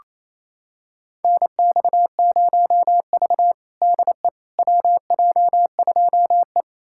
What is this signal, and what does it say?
Morse code, heard as a keyed tone, about 28 wpm, 700 Hz. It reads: NX0V DE WJ2E
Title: USA Callsign DE Callsign - 4 Characters - 28wpm